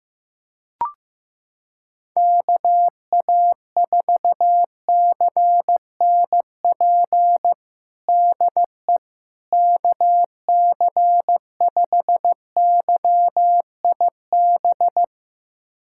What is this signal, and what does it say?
Morse code, heard as a keyed tone, about 15 wpm, 700 Hz. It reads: KA4CNP DE KC5YIB
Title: USA Callsign DE Callsign - 6 Characters - 15wpm